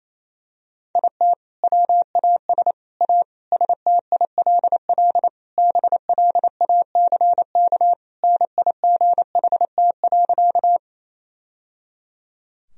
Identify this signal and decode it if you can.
Morse code, heard as a keyed tone, about 28 wpm, 700 Hz. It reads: IT WAS A STILL BLACK NIGHT.
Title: Sentences from Top 1000 Words - Review Part 2 - 28wpm